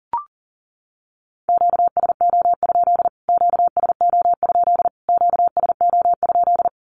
Morse code, heard as a keyed tone, about 40 wpm, 700 Hz. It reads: QSO? QSO? QSO?
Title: Q-Codes Repeated 3x 40wpm